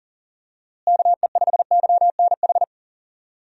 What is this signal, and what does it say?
Morse code, heard as a keyed tone, about 40 wpm, 700 Hz. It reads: KE5YNH
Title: USA Call Signs - 6 Characters - Repeated 3x - 40wpm